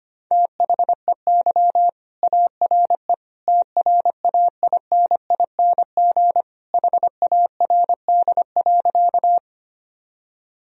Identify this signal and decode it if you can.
Morse code, heard as a keyed tone, about 25 wpm, 700 Hz. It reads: THEY ARE TRAINING HARD.
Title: Sentences from Top 1000 Words - Review Part 4 - 25wpm